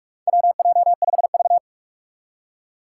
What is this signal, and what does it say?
Morse code, heard as a keyed tone, about 41 wpm, 700 Hz. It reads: WJ5V
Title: Four Challenging Letter-Number Combinations 45wpm